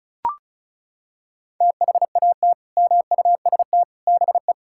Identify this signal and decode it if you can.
Morse code, heard as a keyed tone, about 34 wpm, 700 Hz. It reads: THAT MUST BE
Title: Sentences from Top 1000 Words - Review Part 5 - 35wpm